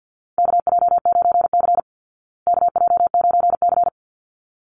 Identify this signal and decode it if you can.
Morse code, heard as a keyed tone, about 50 wpm, 700 Hz. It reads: KJ9C KJ9C
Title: Call Signs Repeated 3x - 50wpm